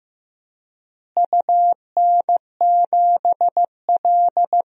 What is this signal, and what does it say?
Morse code, heard as a keyed tone, about 15 wpm, 700 Hz. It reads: UN7L
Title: DXCC-100 Call Signs - 4 Characters - Repeated 2x - 15wpm